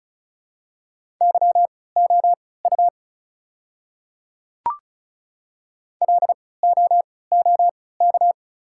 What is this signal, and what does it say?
Morse code, heard as a keyed tone, about 36 wpm, spaced as for 20 wpm, 690 Hz. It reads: YOU LOOK
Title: 3x Character Spacing - Top 500 Words - 35wpm